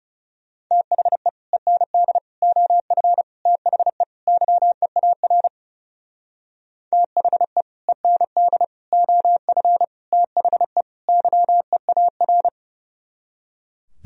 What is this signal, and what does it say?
Morse code, heard as a keyed tone, about 37 wpm, 700 Hz. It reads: THE END OF THE YEAR THE END OF THE YEAR
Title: Speed-Racing - Sets of 5 Words 30wpm